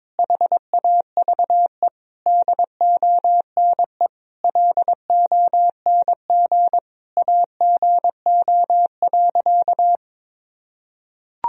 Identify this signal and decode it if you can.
Morse code, heard as a keyed tone, about 22 wpm, 700 Hz. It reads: HAVE DONE LONG AGO.
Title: Sentences from Top 1000 Words - Review Part 3 - 22wpm